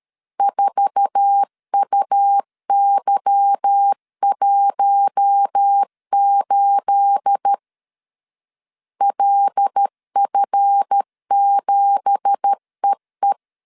Morse code, heard as a keyed tone, about 13 wpm, 800 Hz. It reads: 4UY18 LF7EE
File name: CW / DET / 1 9 1500C S A 12wpm 3800